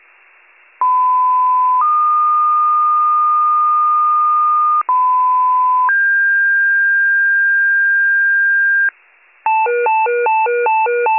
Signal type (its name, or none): two-tone page